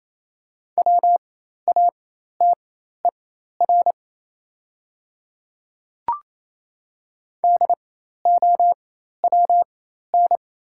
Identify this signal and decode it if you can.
Morse code, heard as a keyed tone, about 28 wpm, spaced as for 13 wpm, 700 Hz. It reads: WATER DOWN
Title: 4x Character Spacing - Top 100 Words - 28wpm